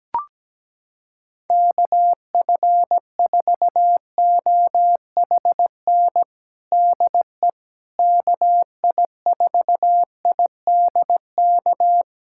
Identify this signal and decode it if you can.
Morse code, heard as a keyed tone, about 17 wpm, 700 Hz. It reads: KF4OHN DE KI4IDK
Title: USA Callsign DE Callsign - 6 Characters - 17wpm